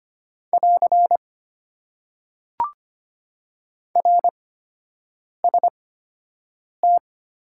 Morse code, heard as a keyed tone, about 25 wpm, spaced as for 7 wpm, 700 Hz. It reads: <AR> RST
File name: Top 90 QSO Elements - 8x Character Spacing - 25wpm